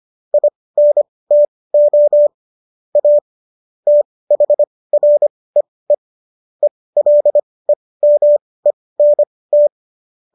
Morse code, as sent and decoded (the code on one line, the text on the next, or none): .. -. - --- / .- / - .... .-. . / . .-.. -- . -. -
INTO A THREE ELEMENT